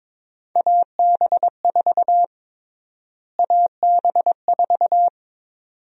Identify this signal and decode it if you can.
Morse code, heard as a keyed tone, about 22 wpm, 700 Hz. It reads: AB4 AB4
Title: USA Call Sign Prefixes - Repeated 2x - 22wpm